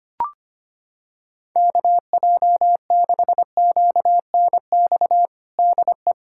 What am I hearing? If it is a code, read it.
Morse code, heard as a keyed tone, about 25 wpm, 700 Hz. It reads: KJ6QNX DE
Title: USA Callsign DE Callsign - 6 Characters - 25wpm